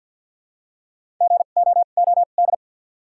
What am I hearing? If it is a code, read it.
Morse code, heard as a keyed tone, about 53 wpm, spaced as for 37 wpm, 700 Hz. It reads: GOOD
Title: 2x Character Spacing - Top 200 Words - 50wpm